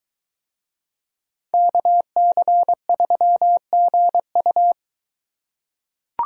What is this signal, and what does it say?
Morse code, heard as a keyed tone, about 21 wpm, 700 Hz. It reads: KC3GU